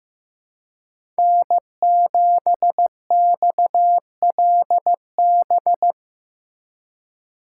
Morse code, heard as a keyed tone, about 15 wpm, 700 Hz. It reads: N7XLB
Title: USA Call Signs - 4x Character Spacing - 15wpm